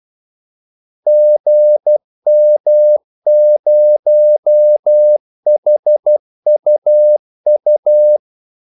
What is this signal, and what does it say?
Morse code, heard as a keyed tone, about 12 wpm, 600 Hz. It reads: GM0HUU
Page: 2 ekstra ordmellemrum mellem kaldesignalerne